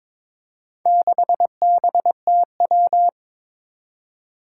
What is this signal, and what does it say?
Morse code, heard as a keyed tone, about 22 wpm, 700 Hz. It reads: 6BTW
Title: Four Challenging Letter-Number Combinations 22wpm